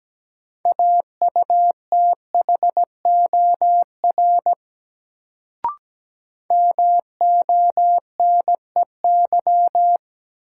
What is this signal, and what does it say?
Morse code, heard as a keyed tone, about 17 wpm, 700 Hz. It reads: AUTHOR MONEY
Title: Sets of 4 Words - New Word Drill - 17wpm